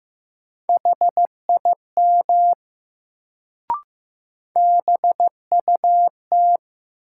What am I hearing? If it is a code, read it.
Morse code, heard as a keyed tone, about 15 wpm, 700 Hz. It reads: HIM BUT
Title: Sets of 2 Words - New Word Drill - 15wpm